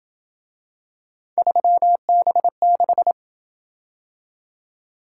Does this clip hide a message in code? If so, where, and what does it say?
Yes: Morse code from 1.4 to 3.1 s, heard as a keyed tone, about 27 wpm, 700 Hz. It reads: 3B6